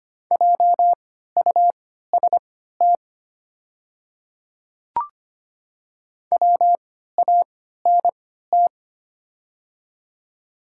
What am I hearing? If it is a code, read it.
Morse code, heard as a keyed tone, about 25 wpm, spaced as for 14 wpm, 700 Hz. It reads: JUST WANT
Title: 3x Character Spacing - Top 200 Words - 25wpm